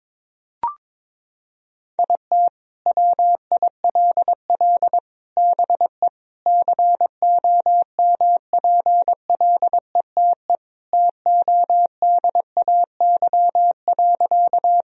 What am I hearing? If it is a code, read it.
Morse code, heard as a keyed tone, about 22 wpm, 700 Hz. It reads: IT WILL BE COMPLETE TODAY.
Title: Sentences from Top 1000 Words - Review Part 2 - 22wpm